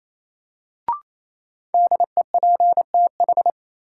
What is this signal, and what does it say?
Morse code, heard as a keyed tone, about 28 wpm, 700 Hz. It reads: DEPTH
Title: Sentences from Top 1200 Words - New Word Drill - 28wpm